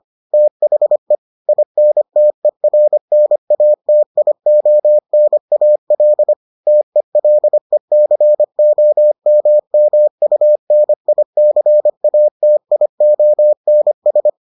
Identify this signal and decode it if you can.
Morse code, heard as a keyed tone, about 25 wpm, 600 Hz. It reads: THE INTERNATIONAL TELECOMMUNICATIONS